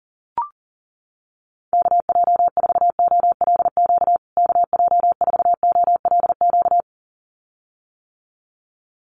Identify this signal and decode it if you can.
Morse code, heard as a keyed tone, about 40 wpm, 700 Hz. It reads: KJ4OLQ KJ4OLQ
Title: USA Call Signs Repeated 2x 40wpm